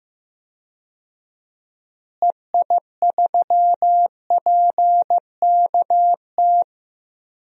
Morse code, heard as a keyed tone, about 15 wpm, 700 Hz. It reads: EI3PKT
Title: DXCC-100 Call Signs - 6 Characters - Repeated 2x - 15wpm